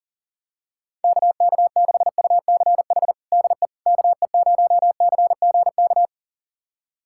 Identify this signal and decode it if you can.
Morse code, heard as a keyed tone, about 40 wpm, 700 Hz. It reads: KK6UCH DE KE0CGK
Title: USA Callsign DE Callsign 40wpm